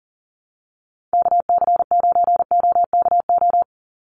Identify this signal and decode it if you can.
Morse code, heard as a keyed tone, about 40 wpm, 700 Hz. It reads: KC9OKO
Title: USA Call Signs 40wpm